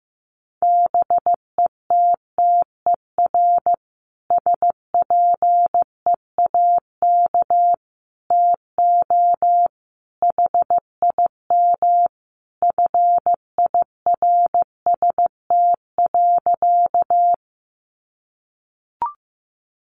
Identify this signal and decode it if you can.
Morse code, heard as a keyed tone, about 15 wpm, 700 Hz. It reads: BETTER SPEAK TO HIM FIRST.
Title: Sentences from Top 1100 Words 15wpm